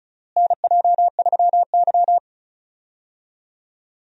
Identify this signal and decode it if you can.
Morse code, heard as a keyed tone, about 36 wpm, 700 Hz. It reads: NJ3Y
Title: USA Call Signs - 4 Characters - Repeated 3x - 35wpm